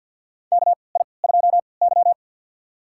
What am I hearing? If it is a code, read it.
Morse code, heard as a keyed tone, about 50 wpm, spaced as for 28 wpm, 700 Hz. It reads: KI2Y
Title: USA Call Signs - 4 Characters - 3x Character Spacing - 50wpm